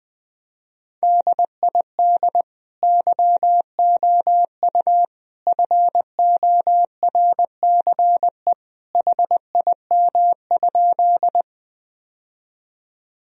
Morse code, heard as a keyed tone, about 19 wpm, 700 Hz. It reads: DID YOU FORCE HIM?
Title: Sentences from Top 1000 Words - Review Part 3 - 20wpm